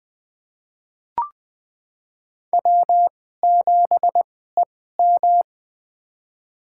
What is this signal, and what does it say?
Morse code, heard as a keyed tone, about 20 wpm, spaced as for 14 wpm, 700 Hz. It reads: W7EM
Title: USA Call Signs - 4 Characters - 2x Character Spacing - 20wpm